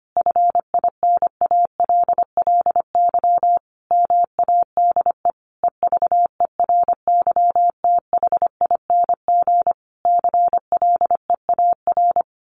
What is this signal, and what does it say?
Morse code, heard as a keyed tone, about 25 wpm, 700 Hz. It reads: FINALLY MADE EVERYTHING CLEAR
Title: Sentences from Top 1300 Words 25wpm